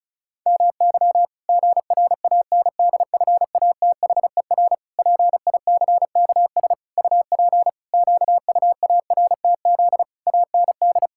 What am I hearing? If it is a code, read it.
Morse code, heard as a keyed tone, about 36 wpm, 700 Hz. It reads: MY GRANDFATHER PICKS UP QUARTZ AND
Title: Pangram Challenge v2 - 35wpm